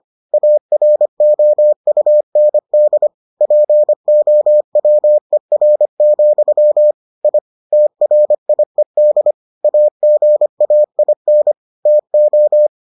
Morse code, heard as a keyed tone, about 25 wpm, 600 Hz. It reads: AROUND POWER, I TRIED AGAIN TO